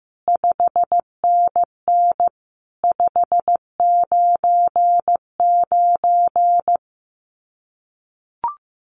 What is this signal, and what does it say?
Morse code, heard as a keyed tone, about 15 wpm, 700 Hz. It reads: 5NN 599
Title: Common QSO Phrases 15wpm